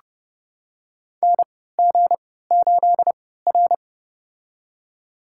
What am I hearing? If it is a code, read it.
Morse code, heard as a keyed tone, about 31 wpm, spaced as for 17 wpm, 700 Hz. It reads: NG8R
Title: USA Call Signs - 3x Character Spacing - 30wpm